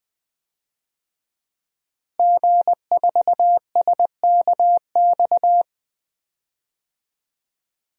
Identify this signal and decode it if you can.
Morse code, heard as a keyed tone, about 20 wpm, 700 Hz. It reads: G4SKX
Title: DXCC-100 Call Signs - 5 Characters - Repeated 2x - 20wpm